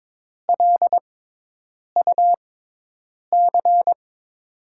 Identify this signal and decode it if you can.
Morse code, heard as a keyed tone, about 22 wpm, spaced as for 8 wpm, 700 Hz. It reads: LUC
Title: USA Call Signs - 5 Characters - 6x Character Spacing - 22wpm